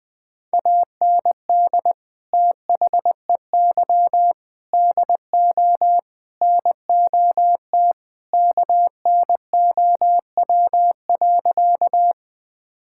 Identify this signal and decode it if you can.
Morse code, heard as a keyed tone, about 21 wpm, 700 Hz. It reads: AND THEY DO NOT KNOW.
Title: Sentences with 4-Word Phrases - 20wpm